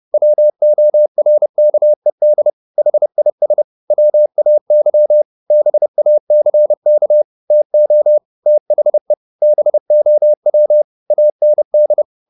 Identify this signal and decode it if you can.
Morse code, heard as a keyed tone, about 30 wpm, 600 Hz. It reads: WORKED HIS WAY BACK TO THE BOW AND
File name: the old man and the sea / the old man and the sea 1 30wpm